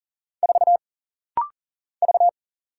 Morse code, heard as a keyed tone, about 40 wpm, 700 Hz. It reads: 4 V
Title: Single Letter-Number - Rapid-Fire 40wpm